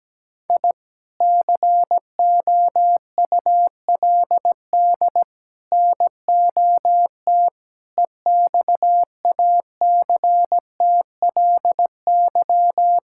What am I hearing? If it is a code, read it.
Morse code, heard as a keyed tone, about 17 wpm, 700 Hz. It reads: I COULD NOT EXACTLY
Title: Sentences from Top 1000 Words - Review Part 6 - 17wpm